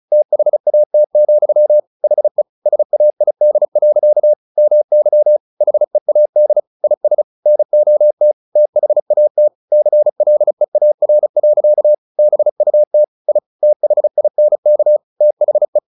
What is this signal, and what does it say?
Morse code, heard as a keyed tone, about 34 wpm, 600 Hz. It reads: THAT, HE SAID. MY HEAD IS NOT THAT CLEAR. BUT I THINK THE
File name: the old man and the sea 2 35wpm